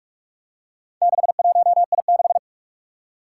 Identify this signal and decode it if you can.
Morse code, heard as a keyed tone, about 44 wpm, 700 Hz. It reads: 61I6
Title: Four Challenging Letter-Number Combinations 45wpm